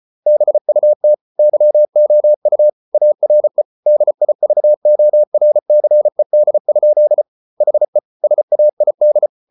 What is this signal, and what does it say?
Morse code, heard as a keyed tone, about 34 wpm, 600 Hz. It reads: BUT YOU ARE DIVORCED? HE SAID